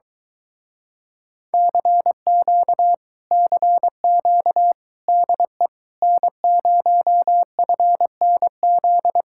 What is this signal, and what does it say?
Morse code, heard as a keyed tone, about 23 wpm, 700 Hz. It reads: CQ CQ DE N0FNZ